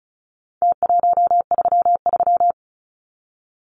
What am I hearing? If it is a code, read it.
Morse code, heard as a keyed tone, about 34 wpm, 700 Hz. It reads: T133
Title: Numbers - Contest Style - 4-Digit 35wpm